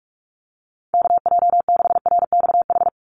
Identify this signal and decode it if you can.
Morse code, heard as a keyed tone, about 44 wpm, 690 Hz. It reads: KJ6RXH
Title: USA Call Signs 45wpm